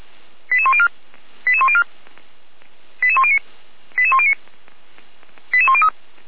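Five-tone selective call (ZVEI), heard as quick runs of five tones, tones of about 70 ms. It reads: 88104 88104 88109 88109 88103
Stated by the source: ZVEI five-tone call 88192 received by SDR, channel 455, recorded by feuerwehr-orlamuende fire brigade server.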